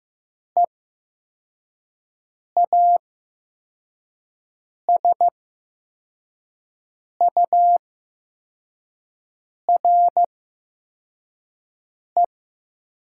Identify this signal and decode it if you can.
Morse code, heard as a keyed tone, about 15 wpm, spaced as for 4 wpm, 700 Hz. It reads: EASURE
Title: 8x Character Spacing - Top 500 Words - 15wpm